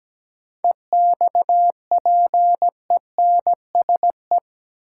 Morse code, heard as a keyed tone, about 17 wpm, 700 Hz. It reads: EXPENSE
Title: Sets of 4 Words - New Word Drill - 17wpm